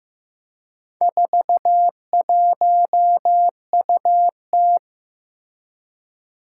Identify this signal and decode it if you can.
Morse code, heard as a keyed tone, about 15 wpm, 700 Hz. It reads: 41UT
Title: Four Challenging Letter-Number Combinations 15wpm